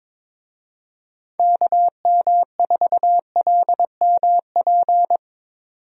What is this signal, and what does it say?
Morse code, heard as a keyed tone, about 22 wpm, 700 Hz. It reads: KM4LMP
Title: USA Call Signs - 6 Characters - 22wpm